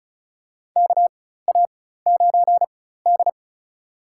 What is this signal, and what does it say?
Morse code, heard as a keyed tone, about 36 wpm, spaced as for 16 wpm, 700 Hz. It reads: KA9D
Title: USA Call Signs - 4 Characters - 4x Character Spacing - 35wpm